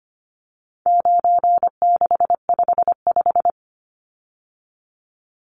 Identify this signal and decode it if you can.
Morse code, heard as a keyed tone, about 25 wpm, 700 Hz. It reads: 9655
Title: Numbers - 4-Digit 25wpm